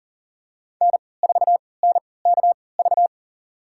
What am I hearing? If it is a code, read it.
Morse code, heard as a keyed tone, about 38 wpm, spaced as for 22 wpm, 700 Hz. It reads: N4NKV
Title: USA Call Signs - 5 Characters - 3x Character Spacing - 40wpm